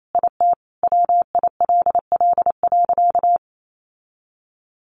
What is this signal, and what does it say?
Morse code, heard as a keyed tone, about 28 wpm, 700 Hz. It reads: IT WILL.
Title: Sentences from Top 1500 Words 28wpm